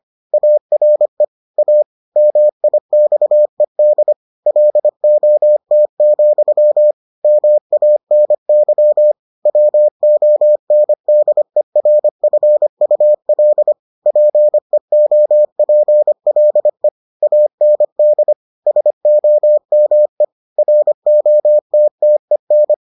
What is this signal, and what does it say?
Morse code, heard as a keyed tone, about 25 wpm, 600 Hz. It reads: ARE A MIXED LOT, MANY WONDERFUL PEOPLE AND SOME ROTTEN